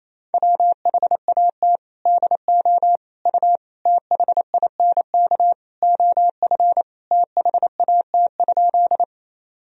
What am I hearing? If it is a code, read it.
Morse code, heard as a keyed tone, about 28 wpm, 700 Hz. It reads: WHAT DO U THINK OF THAT?
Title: Sentences from Top 200 Words and Abbreviations 28wpm